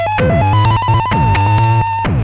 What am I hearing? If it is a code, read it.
There is no code.